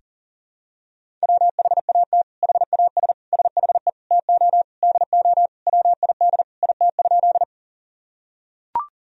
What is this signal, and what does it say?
Morse code, heard as a keyed tone, about 40 wpm, 700 Hz. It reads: WHAT HAS SHE TO DO WID IT?
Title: Sentences from Top 100 Words and Abbreviations 40wpm